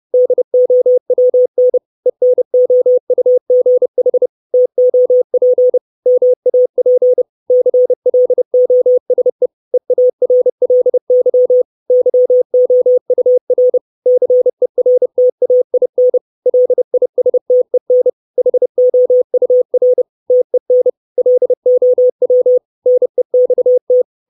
Morse code, heard as a keyed tone, about 30 wpm, 500 Hz. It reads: DOWN ENOUGH TOP MAP CLOSE EARLY YOUR CERTAIN LISTEN HOUR TEN LOW NEXT
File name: 632 Words 30WPM